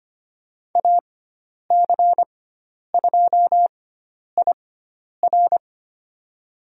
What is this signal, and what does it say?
Morse code, heard as a keyed tone, about 25 wpm, spaced as for 10 wpm, 700 Hz. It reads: AC2IR
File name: Lesson 16g - Letter c - Call Signs 25wpm